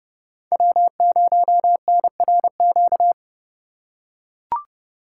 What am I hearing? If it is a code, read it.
Morse code, heard as a keyed tone, about 30 wpm, 700 Hz. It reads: W0NRQ
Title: USA Call Signs - 5 Characters - 2x Character Spacing - 30wpm